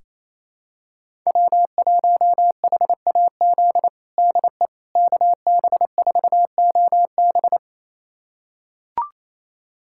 Morse code, heard as a keyed tone, about 28 wpm, 700 Hz. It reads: W1HAZ DE KB4OB
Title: USA Callsign DE Callsign - 5 Characters - 28wpm